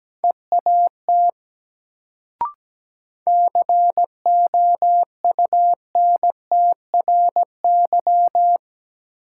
Morse code, heard as a keyed tone, about 17 wpm, 700 Hz. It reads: EAT COUNTRY
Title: Sentences from Top 300 Words - New Word Drill - 17wpm